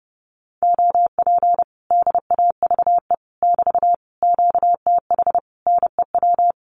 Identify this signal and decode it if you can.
Morse code, heard as a keyed tone, about 29 wpm, 700 Hz. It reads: OP DAVE <BT> QTH NEW
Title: Common QSO Phrases Encore 30wpm